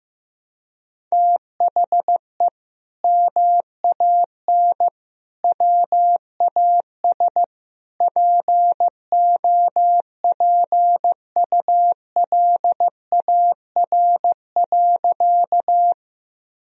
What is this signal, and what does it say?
Morse code, heard as a keyed tone, about 16 wpm, 700 Hz. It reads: THE MAN WAS POPULAR.
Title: Sentences from Top 1000 Words - Review Part 3 - 15wpm